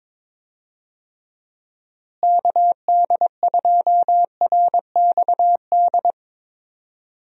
Morse code, heard as a keyed tone, about 22 wpm, 700 Hz. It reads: KD2RXD
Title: USA Call Signs - 6 Characters - Repeated 2x - 22wpm